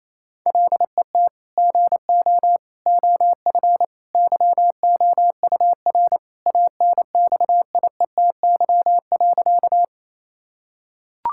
Morse code, heard as a keyed tone, about 28 wpm, 700 Hz. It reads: LET GO OF YOUR ANXIETY.
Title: Sentences from Top 1600 Words 28wpm